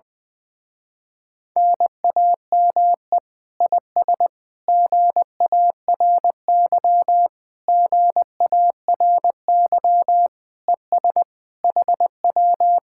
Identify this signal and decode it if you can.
Morse code, heard as a keyed tone, about 20 wpm, 700 Hz. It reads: NAME IS GARY GARY ES HW
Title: Speed-Racing Common QSO Phrases Encore - 17wpm